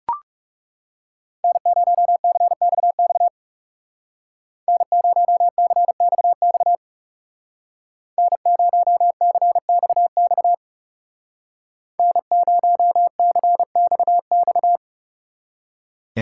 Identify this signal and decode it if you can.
Morse code, heard as a keyed tone, about 39 wpm, 700 Hz. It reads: N0CXX N0CXX N0CXX N0CXX